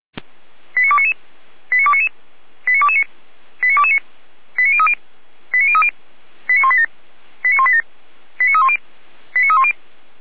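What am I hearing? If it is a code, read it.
Five-tone selective call (ZVEI), heard as quick runs of five tones, tones of about 80 ms. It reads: 89299 89299 89229 89229 89030 89030 89187 89187 89310 89310
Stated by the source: ZVEI five-tone call received via SDR, from the audio server of Feuerwehr Orlamünde.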